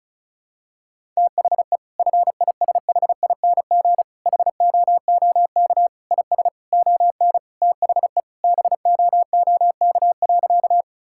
Morse code, heard as a keyed tone, about 36 wpm, 700 Hz. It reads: THE FISHING HOOK IS ON THE BOOK.
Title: Sentences from Top 1600 Words 35wpm